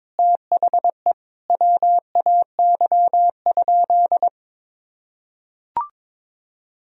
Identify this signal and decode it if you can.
Morse code, heard as a keyed tone, about 22 wpm, 700 Hz. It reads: THE WAY?
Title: Sentences from Top 1000 Words - Review Part 2 - 22wpm